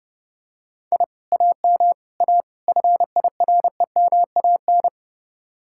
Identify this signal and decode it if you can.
Morse code, heard as a keyed tone, about 32 wpm, 700 Hz. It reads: I AM A FIREMAN